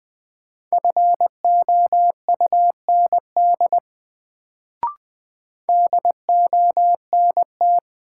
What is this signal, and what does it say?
Morse code, heard as a keyed tone, about 20 wpm, 700 Hz. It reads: FOUND DONT
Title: Sentences from Top 300 Words - New Word Drill - 20wpm